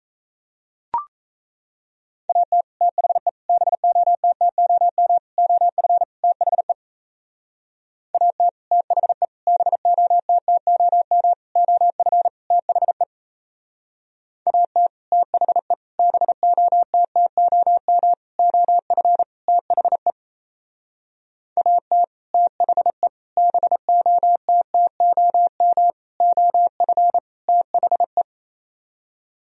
Morse code, heard as a keyed tone, about 33 wpm, 700 Hz. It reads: AT THE BOTTOM OF THE AT THE BOTTOM OF THE AT THE BOTTOM OF THE AT THE BOTTOM OF THE
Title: Speed-Racing - Sets of 5 Words 28wpm